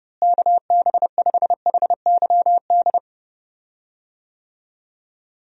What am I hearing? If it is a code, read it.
Morse code, heard as a keyed tone, about 30 wpm, 700 Hz. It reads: KB5HYD
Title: USA Call Signs - 6 Characters - Repeated 2x - 30wpm